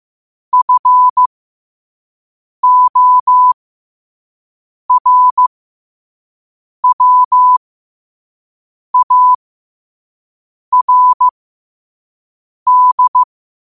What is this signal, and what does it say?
Morse code, heard as a keyed tone, about 15 wpm, spaced as for 5 wpm, 1000 Hz. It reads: FORWARD
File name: qotdfiles / 5wpm